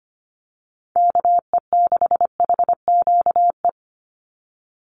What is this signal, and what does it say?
Morse code, heard as a keyed tone, about 25 wpm, 700 Hz. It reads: KE6HQE
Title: USA Call Signs 25wpm